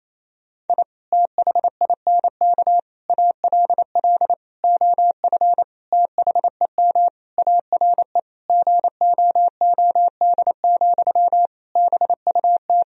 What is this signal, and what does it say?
Morse code, heard as a keyed tone, about 28 wpm, 700 Hz. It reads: I THINK ALL OF THEM ARE GOOD, BUT
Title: Sentences from Top 1000 Words - Review Part 4 - 28wpm